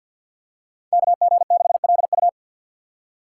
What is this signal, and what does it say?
Morse code, heard as a keyed tone, about 47 wpm, 700 Hz. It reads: KG6LU
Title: USA Call Signs - 5 Characters - 2x Character Spacing - 50wpm